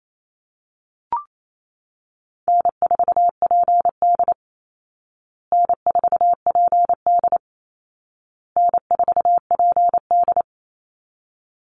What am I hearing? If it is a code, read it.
Morse code, heard as a keyed tone, about 28 wpm, 700 Hz. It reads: N4PD N4PD N4PD
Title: Easy Call Signs Repeated 3x - 28wpm